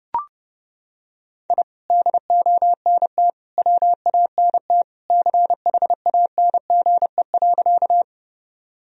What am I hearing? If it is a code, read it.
Morse code, heard as a keyed tone, about 30 wpm, 700 Hz. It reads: I DONT WANT CHANGE.
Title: Sentences from Top 1000 Words - Review Part 1 - 30wpm